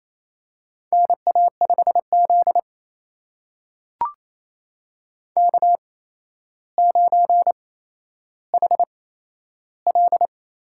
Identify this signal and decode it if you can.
Morse code, heard as a keyed tone, about 27 wpm, spaced as for 12 wpm, 700 Hz. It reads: NA5Z K9HL
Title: USA Call Signs - 4 Characters - 8x Character Spacing - 28wpm